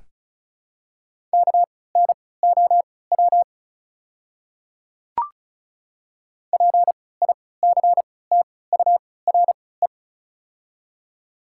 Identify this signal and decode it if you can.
Morse code, heard as a keyed tone, about 36 wpm, spaced as for 20 wpm, 700 Hz. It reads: KNOW PICTURE
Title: 3x Character Spacing - Top 200 Words - 35wpm